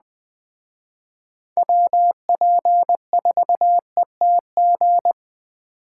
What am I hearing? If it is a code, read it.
Morse code, heard as a keyed tone, about 20 wpm, 700 Hz. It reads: WP4ETG
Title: Speed Racing - USA 2x3 Call Signs 17wpm